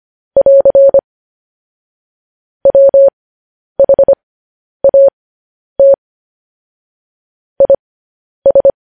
Morse code, heard as a keyed tone, about 25 wpm, spaced as for 10 wpm, 550 Hz. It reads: <AR> WHAT IS
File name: podcast / News headlines-1x-25x10